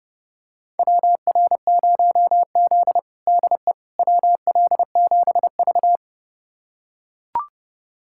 Morse code, heard as a keyed tone, about 32 wpm, 700 Hz. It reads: WR0Z DE WL7V
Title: USA Callsign DE Callsign - 4 Characters - 30wpm